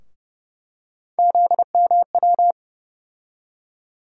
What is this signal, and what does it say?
Morse code, heard as a keyed tone, about 31 wpm, 700 Hz. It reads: ZMW
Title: USA Call Sign Postfixes - 3 Letters 30wpm